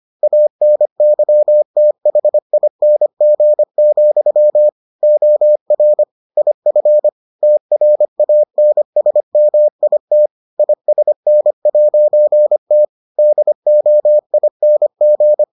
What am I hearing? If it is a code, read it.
Morse code, heard as a keyed tone, about 25 wpm, 600 Hz. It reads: ANYTHING, OR IF TRANSMIT ISN'T DOING